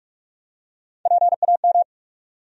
Morse code, heard as a keyed tone, about 46 wpm, 700 Hz. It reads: PAM
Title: USA Common Names - Max 5 Letters - 45wpm